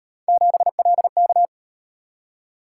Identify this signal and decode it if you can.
Morse code, heard as a keyed tone, about 38 wpm, 700 Hz. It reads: 7LK